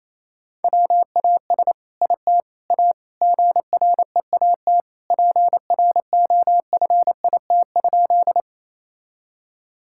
Morse code, heard as a keyed tone, about 28 wpm, 700 Hz. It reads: WAS IT A GREAT PROFIT?
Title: Sentences from Top 1000 Words - Review Part 5 - 28wpm